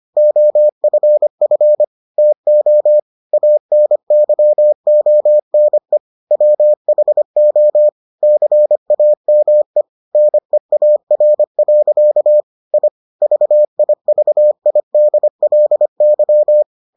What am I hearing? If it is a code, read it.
Morse code, heard as a keyed tone, about 25 wpm, 600 Hz. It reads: OFF TO ANYONE WHO CAME NEAR. I VIVIDLY